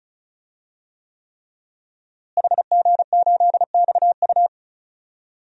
Morse code, heard as a keyed tone, about 36 wpm, 700 Hz. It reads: HG8XU
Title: DXCC-100 Call Signs - 5 Characters - Repeated 3x - 35wpm